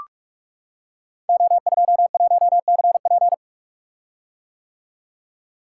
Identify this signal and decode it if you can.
Morse code, heard as a keyed tone, about 44 wpm, 700 Hz. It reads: O21CP
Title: Code Groups - 5 Letter:Numbers - 45wpm